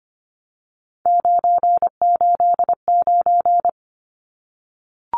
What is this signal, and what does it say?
Morse code, heard as a keyed tone, about 24 wpm, 700 Hz. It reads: 989
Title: Numbers - 3-Digit 25wpm